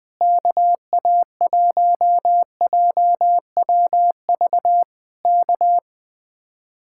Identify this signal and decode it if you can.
Morse code, heard as a keyed tone, about 19 wpm, 700 Hz. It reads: KA1JWV K
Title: Speed-Racing Common QSO Phrases - 20wpm